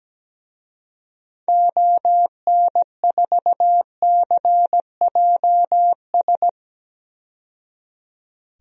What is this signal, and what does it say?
Morse code, heard as a keyed tone, about 17 wpm, 700 Hz. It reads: ON4CJS
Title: DXCC-100 Call Signs - 6 Characters - 17wpm